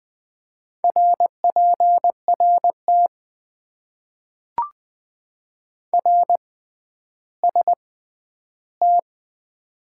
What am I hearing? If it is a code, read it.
Morse code, heard as a keyed tone, about 20 wpm, spaced as for 11 wpm, 700 Hz. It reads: RPRT RST
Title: Top 90 QSO Elements - 6x Character Spacing - 20wpm